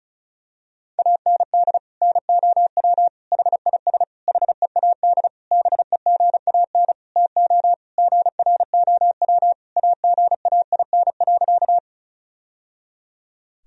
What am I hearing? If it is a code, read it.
Morse code, heard as a keyed tone, about 34 wpm, 700 Hz. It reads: AND NOW HIS HEAD BEGAN TO GROW AGAIN.
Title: Sentences from Top 1000 Words - Review Part 1 - 35wpm